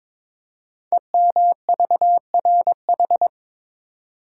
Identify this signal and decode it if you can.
Morse code, heard as a keyed tone, about 23 wpm, 700 Hz. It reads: EMVRH
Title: Code Groups - 5 Letters - 22wpm